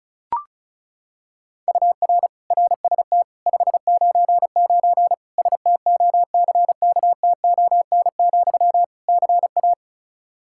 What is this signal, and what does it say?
Morse code, heard as a keyed tone, about 36 wpm, 700 Hz. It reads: UR RST 599 STOCKTON, CA
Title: K1USN SST QSO Practice 35wpm